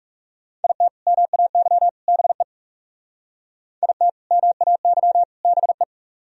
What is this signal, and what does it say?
Morse code, heard as a keyed tone, about 38 wpm, 700 Hz. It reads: IT MAY BE IT MAY BE